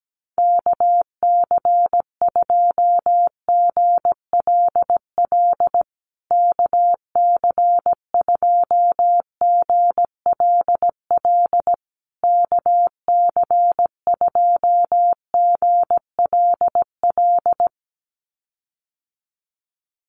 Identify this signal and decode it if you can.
Morse code, heard as a keyed tone, about 17 wpm, 700 Hz. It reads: KC2GLL KC2GLL KC2GLL
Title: USA Call Signs Repeated 3x 17wpm